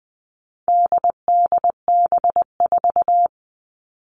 Morse code, heard as a keyed tone, about 20 wpm, 700 Hz. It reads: DDB4
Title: All Cut Numbers - 4-Digit 20wpm